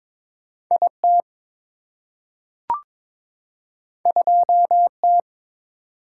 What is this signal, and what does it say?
Morse code, heard as a keyed tone, about 22 wpm, 690 Hz. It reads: IT 2T
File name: practice / 22wpm / Two Letter-Number Combinations - 22wpm